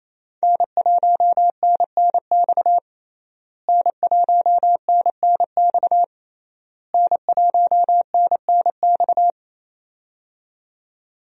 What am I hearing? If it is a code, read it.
Morse code, heard as a keyed tone, about 28 wpm, 700 Hz. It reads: N1NNX N1NNX N1NNX
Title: USA Call Signs - 5 Characters - Repeated 3x - 28wpm